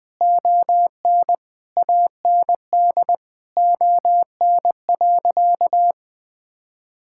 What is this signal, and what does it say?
Morse code, heard as a keyed tone, about 20 wpm, 700 Hz. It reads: ON AND ON.
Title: Sentences with 7-Word Phrases - 20wpm